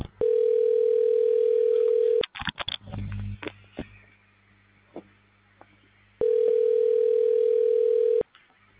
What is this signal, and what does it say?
Ringback tone (North America), 440 + 480 Hz.